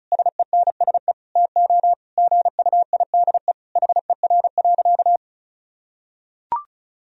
Morse code, heard as a keyed tone, about 34 wpm, 700 Hz. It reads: SENSE TO GUIDE HER.
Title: Sentences from Top 1000 Words - Review Part 4 - 35wpm